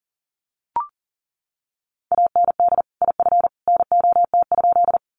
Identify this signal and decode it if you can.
Morse code, heard as a keyed tone, about 40 wpm, 700 Hz. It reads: AND IF NOT?
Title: Sentences from Top 600 Words 40wpm